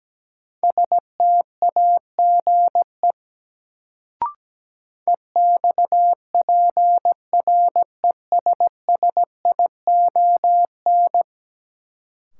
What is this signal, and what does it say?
Morse code, heard as a keyed tone, about 17 wpm, 700 Hz. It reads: STAGE EXPRESSION
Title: Sentences from Top 1200 Words - New Word Drill Alt - 17wpm